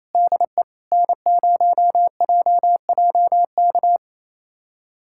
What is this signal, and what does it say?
Morse code, heard as a keyed tone, about 28 wpm, 700 Hz. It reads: DE N0JJK
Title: USA Callsign DE Callsign - 5 Characters - 28wpm